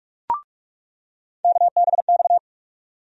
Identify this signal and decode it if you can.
Morse code, heard as a keyed tone, about 42 wpm, 700 Hz. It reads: KBX
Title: USA Call Sign Postfixes - 3 Letters 45wpm